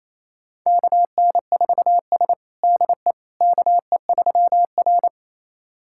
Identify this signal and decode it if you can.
Morse code, heard as a keyed tone, about 27 wpm, 700 Hz. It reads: KN4S DE KE3R
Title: USA Callsign DE Callsign - 4 Characters - 28wpm